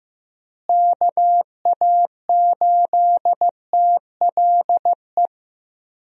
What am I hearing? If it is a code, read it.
Morse code, heard as a keyed tone, about 16 wpm, 700 Hz. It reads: KA8TLE
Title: USA Call Signs - 6 Characters - Repeated 2x - 15wpm